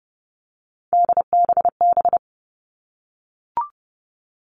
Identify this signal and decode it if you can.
Morse code, heard as a keyed tone, about 30 wpm, 700 Hz. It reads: DBB
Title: All Cut Numbers - 3-Digit 30wpm